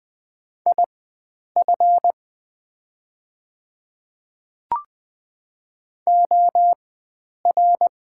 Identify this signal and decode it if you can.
Morse code, heard as a keyed tone, about 20 wpm, spaced as for 9 wpm, 700 Hz. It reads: IF OR
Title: 4x Character Spacing - Top 100 Words - 20wpm